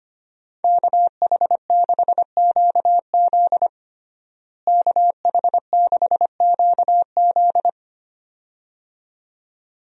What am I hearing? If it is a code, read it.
Morse code, heard as a keyed tone, about 25 wpm, 700 Hz. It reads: KH6QZ KH6QZ
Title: USA Call Signs - 5 Characters - Repeated 2x - 25wpm